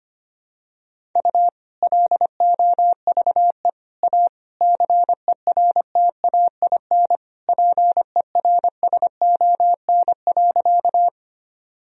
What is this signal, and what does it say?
Morse code, heard as a keyed tone, about 25 wpm, 700 Hz. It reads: U LOVE A CERTAIN PERSON.
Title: Sentences from Top 500 Words and Abbreviations 25wpm